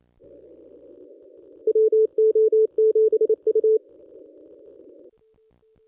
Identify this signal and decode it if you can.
Morse code, heard as a keyed tone, about 28 wpm, 440 Hz. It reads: WO7U
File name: contests / ss-cw-k5tr-op-n5rz -1200